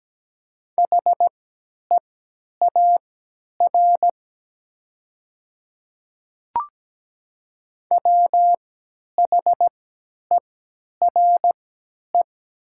Morse code, heard as a keyed tone, about 17 wpm, spaced as for 10 wpm, 700 Hz. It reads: HEAR WHERE
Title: 3x Character Spacing - Top 300 Words - 17wpm